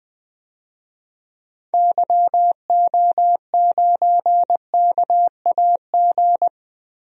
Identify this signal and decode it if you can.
Morse code, heard as a keyed tone, about 21 wpm, 700 Hz. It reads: YO9KAG
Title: DXCC-100 Call Signs - 6 Characters - Repeated 3x - 20wpm